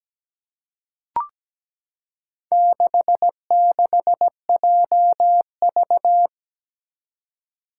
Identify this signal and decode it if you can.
Morse code, heard as a keyed tone, about 17 wpm, 700 Hz. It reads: 66JV
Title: Four Challenging Letter-Number Combinations 17wpm